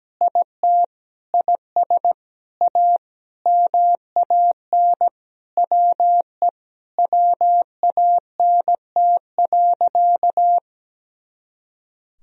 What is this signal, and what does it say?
Morse code, heard as a keyed tone, about 17 wpm, 700 Hz. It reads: IT IS A MAN WE WANT.
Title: Sentences from Top 200 Words and Abbreviations 17wpm